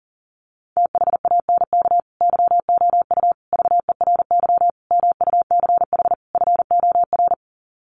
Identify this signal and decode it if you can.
Morse code, heard as a keyed tone, about 40 wpm, 700 Hz. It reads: THANK YOU VERY MUCH FOR
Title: Sets of 5 Words 40wpm